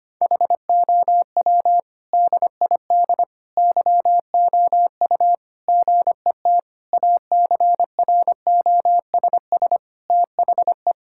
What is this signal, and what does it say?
Morse code, heard as a keyed tone, about 25 wpm, 700 Hz. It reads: HOW DID YOU GET ACROSS THE
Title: Sentences from Top 1000 Words - Review Part 5 - 25wpm